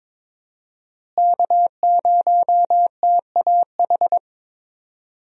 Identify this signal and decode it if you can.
Morse code, heard as a keyed tone, about 22 wpm, 700 Hz. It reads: K0TAH
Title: USA Call Signs - 5 Characters - Repeated 3x - 22wpm